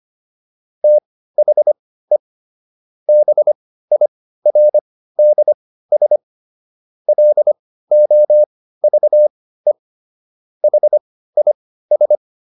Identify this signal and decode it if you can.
Morse code, heard as a keyed tone, about 25 wpm, spaced as for 15 wpm, 600 Hz. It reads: THE BIRDS LOVE HIS